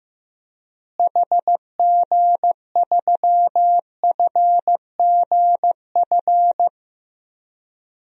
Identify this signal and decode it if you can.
Morse code, heard as a keyed tone, about 15 wpm, 700 Hz. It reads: HG3FGF
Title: DXCC-100 Call Signs - 6 Characters - Repeated 3x - 15wpm